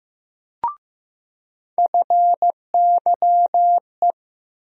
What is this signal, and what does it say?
Morse code, heard as a keyed tone, about 15 wpm, 700 Hz. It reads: FYE
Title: USA Call Sign Postfixes - 3 Letters 15wpm